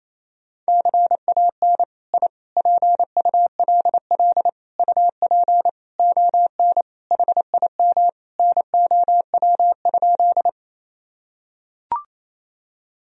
Morse code, heard as a keyed tone, about 28 wpm, 700 Hz. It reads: CAN I PULL UP ON HIM NOW?
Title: Sentences from Top 1000 Words - Review Part 2 - 28wpm